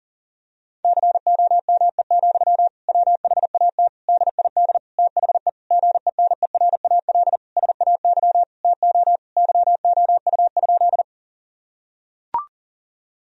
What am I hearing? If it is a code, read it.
Morse code, heard as a keyed tone, about 37 wpm, 700 Hz. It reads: COME, WHAT DID THE GENERAL SAY TO YOU?
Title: Sentences from Top 1000 Words - Review Part 3 - 40wpm